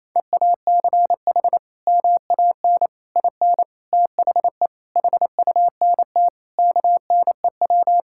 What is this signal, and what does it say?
Morse code, heard as a keyed tone, about 28 wpm, 700 Hz. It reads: EACH MAN IN THE HUNT KNEW
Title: Sentences from Top 1600 Words 28wpm